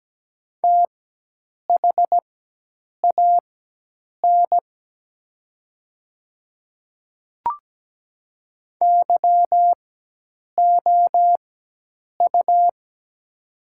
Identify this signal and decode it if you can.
Morse code, heard as a keyed tone, about 17 wpm, spaced as for 8 wpm, 700 Hz. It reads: THAN YOU
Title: 4x Character Spacing - Top 300 Words - 17wpm